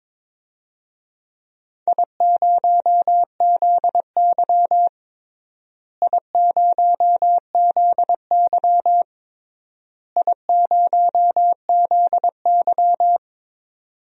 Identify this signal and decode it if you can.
Morse code, heard as a keyed tone, about 22 wpm, 700 Hz. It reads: I0ZY I0ZY I0ZY
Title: DXCC-100 Call Signs - 4 Characters - Repeated 3x - 22wpm